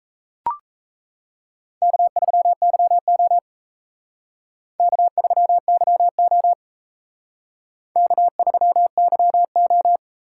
Morse code, heard as a keyed tone, about 34 wpm, 700 Hz. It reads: K3YO K3YO K3YO
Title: Speed Racing - USA 1x2 Call Signs 28wpm